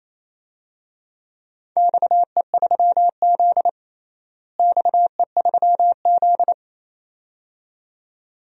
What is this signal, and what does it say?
Morse code, heard as a keyed tone, about 28 wpm, 700 Hz. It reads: XE3Z XE3Z
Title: DXCC-100 Call Signs - 4 Characters - Repeated 2x - 28wpm